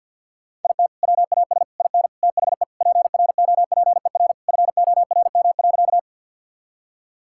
Morse code, heard as a keyed tone, about 50 wpm, 700 Hz. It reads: IT WAS IN THE PROPER FORM.
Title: Sentences from Top 1000 Words - Review Part 5 - 50wpm